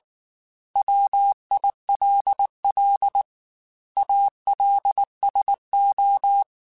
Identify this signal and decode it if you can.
Morse code, heard as a keyed tone, about 19 wpm, 800 Hz. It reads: WILL ALSO